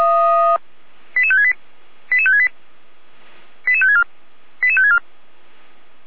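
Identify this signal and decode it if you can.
Five-tone selective call (ZVEI), heard as quick runs of five tones, tones of about 80 ms. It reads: 88568 88568 88564 88564
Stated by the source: ZVEI five-tone selective call 88106 received by SDR.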